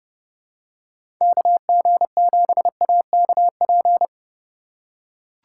Morse code, heard as a keyed tone, about 32 wpm, 700 Hz. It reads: KG7AKP
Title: USA Call Signs - 6 Characters - 2x Character Spacing - 30wpm